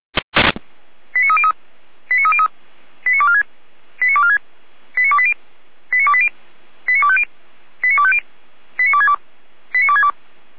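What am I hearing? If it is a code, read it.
Five-tone selective call (ZVEI), heard as quick runs of five tones, tones of about 70 ms. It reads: 89303 89303 89246 89246 89280 89280 89250 89250 89262 89262
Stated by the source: ZVEI five-tone selective call 89220 received with an SDR, archived by Feuerwehr Orlamünde (F85360).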